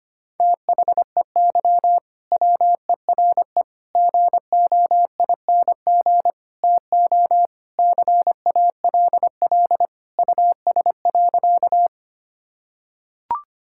Morse code, heard as a keyed tone, about 25 wpm, 700 Hz. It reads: THEY WERE GOING TO CALL US.
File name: practice / 25wpm / Sentences with 4-Word Phrases - 25wpm